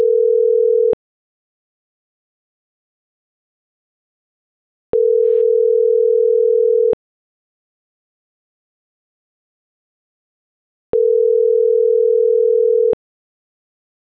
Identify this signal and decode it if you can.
Ringback tone (North America), 440 + 480 Hz.